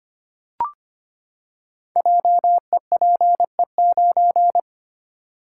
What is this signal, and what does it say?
Morse code, heard as a keyed tone, about 25 wpm, 700 Hz. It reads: JEPE9
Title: Code Groups - 5 Letter:Numbers - 25wpm